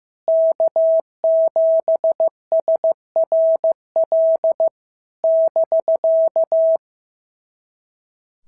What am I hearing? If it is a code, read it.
Morse code, heard as a keyed tone, about 14 wpm, 650 Hz. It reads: K7SRL <BK>
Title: SOTA QSO Practice 15wpm